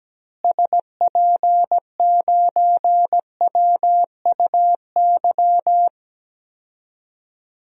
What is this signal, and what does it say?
Morse code, heard as a keyed tone, about 17 wpm, 700 Hz. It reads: SP9WUY
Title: DXCC-100 Call Signs - 6 Characters - Repeated 2x - 17wpm